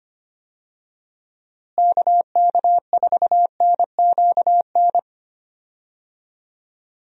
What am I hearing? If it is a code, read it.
Morse code, heard as a keyed tone, about 25 wpm, 700 Hz. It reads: KK4NQN
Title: USA Call Signs - 6 Characters - Repeated 3x - 25wpm